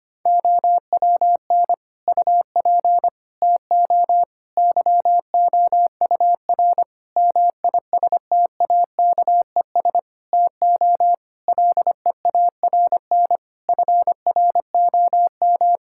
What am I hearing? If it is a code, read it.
Morse code, heard as a keyed tone, about 25 wpm, 700 Hz. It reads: OWN UP TO YOUR MISTAKES TO LEARN FROM
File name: practice / 25wpm / Difficult Sentences with Hints - 25wpm